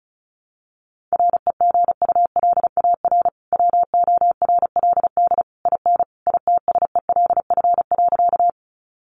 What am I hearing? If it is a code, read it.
Morse code, heard as a keyed tone, about 34 wpm, 700 Hz. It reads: REGULAR WORLD IN ITSELF.
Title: Sentences from Top 900 Words 35wpm